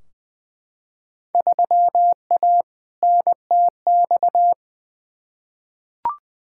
Morse code, heard as a keyed tone, about 21 wpm, 700 Hz. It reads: 3A NTX
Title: Field Day Exchanges - 20wpm